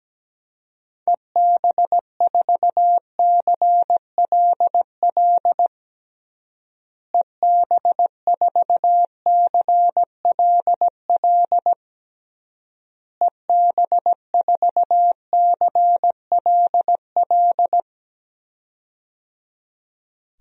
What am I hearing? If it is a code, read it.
Morse code, heard as a keyed tone, about 17 wpm, 700 Hz. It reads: EB4CLL EB4CLL EB4CLL
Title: DXCC-100 Call Signs - 6 Characters - Repeated 3x - 17wpm